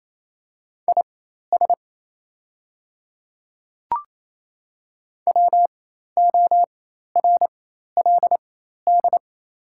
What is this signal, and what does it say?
Morse code, heard as a keyed tone, about 28 wpm, spaced as for 13 wpm, 690 Hz. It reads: IS WORLD
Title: 4x Character Spacing - Top 200 Words - 28wpm